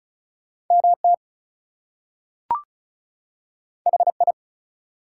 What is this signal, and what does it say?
Morse code, heard as a keyed tone, about 36 wpm, 700 Hz. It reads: MT HI
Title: US States and Canadian Provinces 35wpm